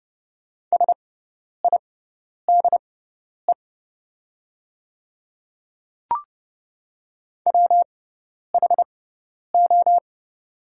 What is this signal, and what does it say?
Morse code, heard as a keyed tone, about 30 wpm, spaced as for 10 wpm, 700 Hz. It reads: SIDE WHO
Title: 6x Character Spacing - Top 100 Words - 30wpm